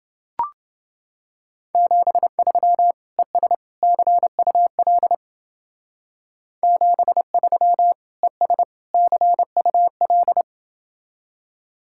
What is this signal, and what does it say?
Morse code, heard as a keyed tone, about 29 wpm, 700 Hz. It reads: 73 ES CUL 73 ES CUL